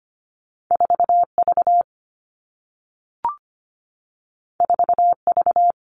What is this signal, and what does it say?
Morse code, heard as a keyed tone, about 25 wpm, 700 Hz. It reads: 4 V 4 V
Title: All Cut Numbers - 2-Digit 25wpm